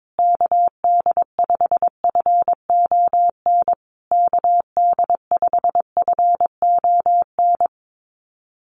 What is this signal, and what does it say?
Morse code, heard as a keyed tone, about 22 wpm, 700 Hz. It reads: KD5FON KD5FON
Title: USA Call Signs Repeated 3x 22wpm